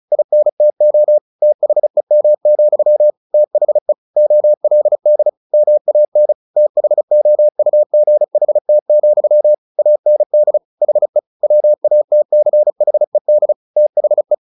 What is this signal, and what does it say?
Morse code, heard as a keyed tone, about 34 wpm, 600 Hz. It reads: INTO THEM, THE OLD MAN THOUGHT, AND HE WATCHED THE
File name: the old man and the sea / the old man and the sea 1 35wpm